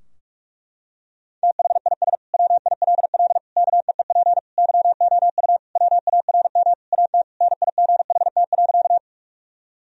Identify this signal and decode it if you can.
Morse code, heard as a keyed tone, about 46 wpm, 700 Hz. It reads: THIS WILL KEEP YOU WARM AT NIGHT.
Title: Sentences from Top 1000 Words - Review Part 2 - 45wpm